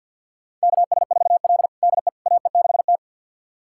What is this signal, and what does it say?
Morse code, heard as a keyed tone, about 46 wpm, 700 Hz. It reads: KS4L DE AE6T
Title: USA Callsign DE Callsign - 4 Characters - 50wpm